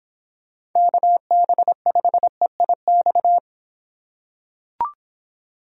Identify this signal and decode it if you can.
Morse code, heard as a keyed tone, about 25 wpm, 700 Hz. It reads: KB5EIX